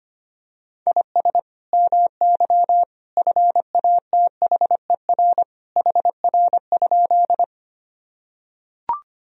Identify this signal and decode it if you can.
Morse code, heard as a keyed tone, about 25 wpm, 700 Hz. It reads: IS MY FATHER HR?
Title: Sentences from Top 300 Words and Abbreviations 25wpm